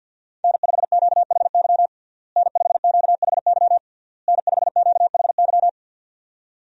Morse code, heard as a keyed tone, about 51 wpm, 700 Hz. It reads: N5QHY N5QHY N5QHY
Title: USA Call Signs - 5 Characters - Repeated 3x - 50wpm